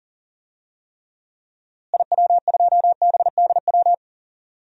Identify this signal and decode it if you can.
Morse code, heard as a keyed tone, about 40 wpm, 700 Hz. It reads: IW2BDW
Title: DXCC-100 Call Signs - 6 Characters - Repeated 3x - 40wpm